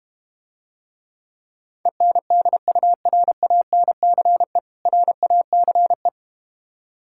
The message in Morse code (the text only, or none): ENDURANCE RACE